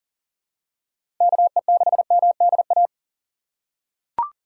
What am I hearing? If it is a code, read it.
Morse code, heard as a keyed tone, about 39 wpm, 700 Hz. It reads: KE6MDA
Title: USA Call Signs - 4x Character Spacing - 40wpm